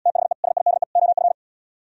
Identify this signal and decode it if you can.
Morse code, heard as a keyed tone, about 74 wpm, 700 Hz. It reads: THE SIDE OF